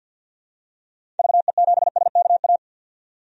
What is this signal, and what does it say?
Morse code, heard as a keyed tone, about 50 wpm, 700 Hz. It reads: VE7SKA
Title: DXCC-100 Call Signs - 6 Characters - 50wpm